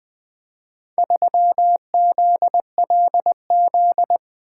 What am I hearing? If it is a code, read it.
Morse code, heard as a keyed tone, about 24 wpm, 700 Hz. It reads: 3ZLZ
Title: Speed-Racing - Four Letter-Number Combinations - 20wpm